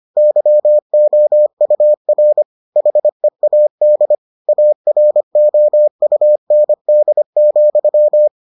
Morse code, heard as a keyed tone, about 25 wpm, 600 Hz. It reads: YOUR HEAD AROUND,